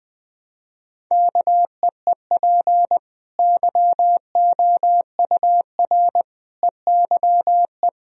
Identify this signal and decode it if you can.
Morse code, heard as a keyed tone, about 21 wpm, 700 Hz. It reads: KEEP YOUR EYE
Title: Sentences from Top 1000 Words - Review Part 4 - 20wpm